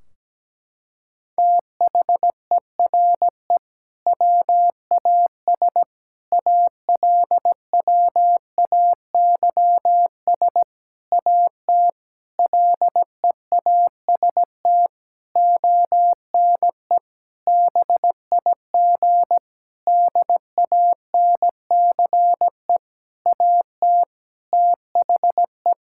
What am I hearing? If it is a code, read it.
Morse code, heard as a keyed tone, about 17 wpm, 700 Hz. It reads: THERE WAS ALWAYS AT LEAST ONE BIG DANCE AT THE
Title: Sentences from Top 1600 Words 17wpm